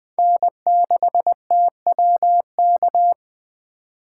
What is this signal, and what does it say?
Morse code, heard as a keyed tone, about 21 wpm, 700 Hz. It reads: N6TWK